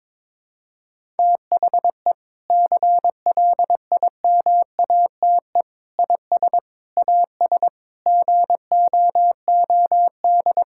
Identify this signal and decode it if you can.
Morse code, heard as a keyed tone, about 22 wpm, 700 Hz. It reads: THE CLIMATE IS AS GOOD
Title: Sentences from Top 1600 Words 22wpm